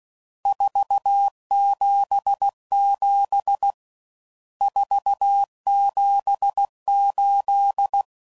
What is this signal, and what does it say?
Morse code, heard as a keyed tone, about 16 wpm, 800 Hz. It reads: 477 478
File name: CW Code Groups @15WPM / RSTs / Various RST Nos